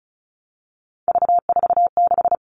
Morse code, heard as a keyed tone, about 34 wpm, 700 Hz. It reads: V46
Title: All Cut Numbers - 3-Digit 35wpm